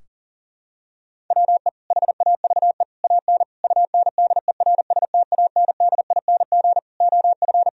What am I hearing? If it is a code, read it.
Morse code, heard as a keyed tone, about 40 wpm, 700 Hz. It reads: WE HAVE AN UNDERSTANDING OF
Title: Sentences from Top 1000 Words - Review Part 3 - 40wpm